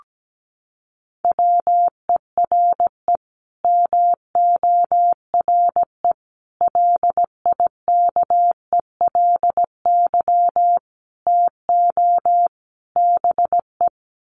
Morse code, heard as a keyed tone, about 17 wpm, 700 Hz. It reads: WERE MORE LIKELY TO BE
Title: Sets of 5 Words Encore2 17wpm